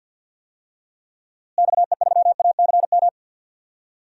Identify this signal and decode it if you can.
Morse code, heard as a keyed tone, about 52 wpm, 700 Hz. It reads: XE3ACM
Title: DXCC-100 Call Signs - 6 Characters - Repeated 3x - 50wpm